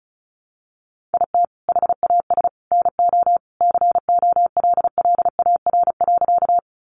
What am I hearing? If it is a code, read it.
Morse code, heard as a keyed tone, about 34 wpm, 700 Hz. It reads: IT HAS NO COLLAR.
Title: Easy Phrases 35wpm